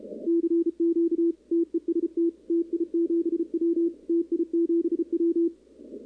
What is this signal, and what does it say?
Morse code, heard as a keyed tone, about 34 wpm, 340 Hz, with radio noise. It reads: CQ TEST TI7W TI7W
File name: contests / 2019-cw-wpx-kt5j-k5pi- -2330 0-11